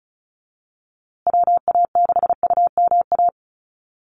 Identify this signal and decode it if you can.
Morse code, heard as a keyed tone, about 34 wpm, 700 Hz. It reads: WA6UMA
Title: USA Call Signs Repeated 3x 35wpm